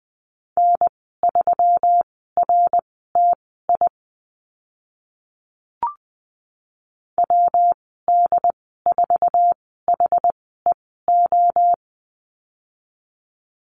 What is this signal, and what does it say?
Morse code, heard as a keyed tone, about 20 wpm, spaced as for 14 wpm, 700 Hz. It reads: N3RTI WD4HEO
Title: 140 Lesson 23g - Letter f - Call Signs 20wpm